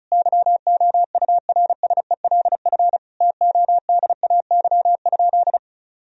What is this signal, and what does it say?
Morse code, heard as a keyed tone, about 36 wpm, 700 Hz. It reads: YOURSELF TODAY?
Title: Sentences from Top 1000 Words - Review Part 4 - 35wpm